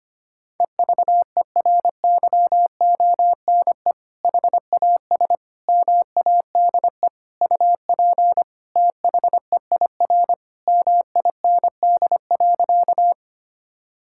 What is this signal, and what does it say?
Morse code, heard as a keyed tone, about 25 wpm, 700 Hz. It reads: EVERYONE HAS MADE UP THEIR MIND.
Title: Sentences from Top 1000 Words - Review Part 5 - 25wpm